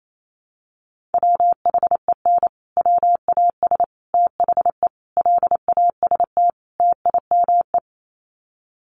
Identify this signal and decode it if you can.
Morse code, heard as a keyed tone, about 28 wpm, 700 Hz. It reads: WHEN WAS THE LAST TIME
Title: Sets of 5 Words Encore 28wpm